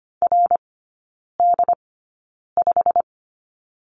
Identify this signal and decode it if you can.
Morse code, heard as a keyed tone, about 25 wpm, spaced as for 9 wpm, 700 Hz. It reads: RD5
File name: Lesson 15e - Number 5 - Three Characters 25wpm